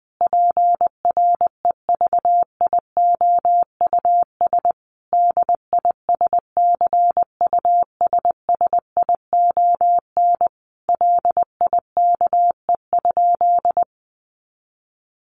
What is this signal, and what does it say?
Morse code, heard as a keyed tone, about 20 wpm, 700 Hz. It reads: PREVIOUS DISCUSSION LIKE?
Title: Sentences from Top 1500 Words 20wpm